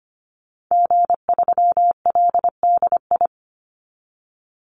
Morse code, heard as a keyed tone, about 25 wpm, 700 Hz. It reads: G3LDI
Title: Call Signs Repeated 2x - 25wpm